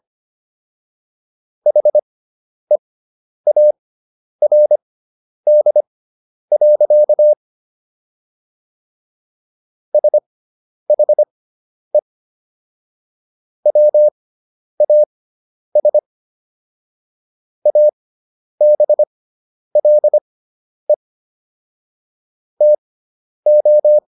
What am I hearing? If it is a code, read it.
Morse code, heard as a keyed tone, about 25 wpm, spaced as for 10 wpm, 600 Hz. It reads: HEARD. SHE WAS ABLE TO